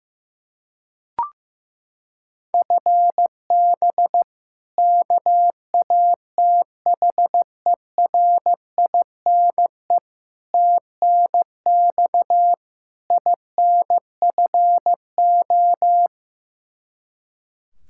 Morse code, heard as a keyed tone, about 15 wpm, 700 Hz. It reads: FB KATHERINE TNX INFO
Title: Practice QSO - Exchange 3 - More Info - 15wpm